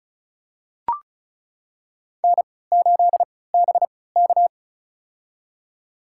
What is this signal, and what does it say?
Morse code, heard as a keyed tone, about 36 wpm, spaced as for 20 wpm, 700 Hz. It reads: N8BK
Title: USA Call Signs - 4 Characters - 3x Character Spacing - 35wpm